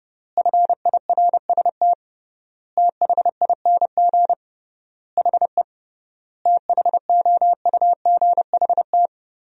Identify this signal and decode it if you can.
Morse code, heard as a keyed tone, about 32 wpm, 700 Hz. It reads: FIRST THING HE THOUGHT
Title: 3x Word-Spacing - Sentences from Top 300 Words - 30wpm